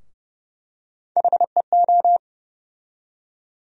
Morse code, heard as a keyed tone, about 32 wpm, 700 Hz. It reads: HEO
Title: USA Call Sign Postfixes - 3 Letters 30wpm